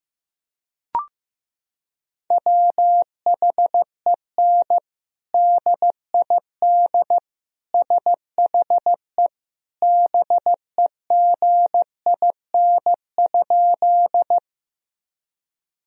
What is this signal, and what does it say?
Morse code, heard as a keyed tone, about 15 wpm, 700 Hz. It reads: WHEN DID SHE BEGIN?
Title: Sentences from Top 1000 Words - Review Part 1 - 15wpm